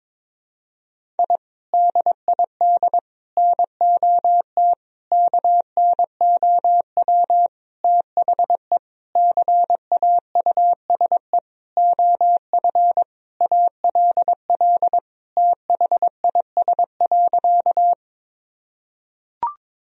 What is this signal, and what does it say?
Morse code, heard as a keyed tone, about 22 wpm, 700 Hz. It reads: I DID NOT KNOW THE CAUSE OF ALL THIS.
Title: Sentences from Top 1000 Words - Review Part 1 - 22wpm